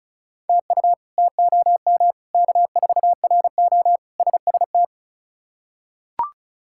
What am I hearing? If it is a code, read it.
Morse code, heard as a keyed tone, about 36 wpm, 700 Hz. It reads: TU TOM K4RO SST
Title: K1USN SST QSO Practice - 35wpm